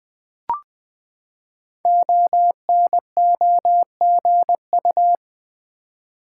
Morse code, heard as a keyed tone, about 20 wpm, 700 Hz. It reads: ONOGU
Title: Code Groups - 5 Letters - 20wpm